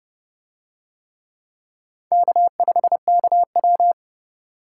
Morse code, heard as a keyed tone, about 28 wpm, 700 Hz. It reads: K5KW